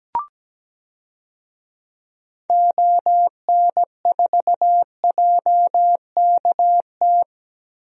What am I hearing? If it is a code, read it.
Morse code, heard as a keyed tone, about 17 wpm, 700 Hz. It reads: ON4JKT
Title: DXCC-100 Call Signs - 6 Characters - Repeated 2x - 17wpm